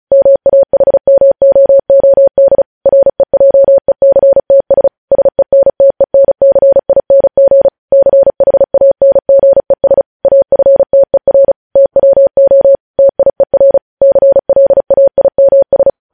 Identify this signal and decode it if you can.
Morse code, heard as a keyed tone, about 36 wpm, 550 Hz. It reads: MAHMOOD REJECTS SENTENCING CHANGES AFTER TWO TIER CLAIMS
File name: podcast / News headlines-1x-35x35